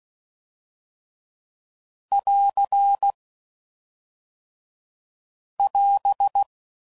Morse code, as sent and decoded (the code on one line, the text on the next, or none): .-.-. / .-...
<AR> <AS>